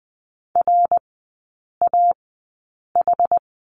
Morse code, heard as a keyed tone, about 20 wpm, spaced as for 8 wpm, 700 Hz. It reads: RAH
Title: 074 Lesson 13e - Letter L - Three Characters 20wpm